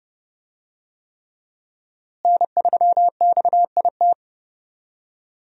Morse code, heard as a keyed tone, about 30 wpm, 700 Hz. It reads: N3XIT
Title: USA Call Signs - 5 Characters - Repeated 3x - 30wpm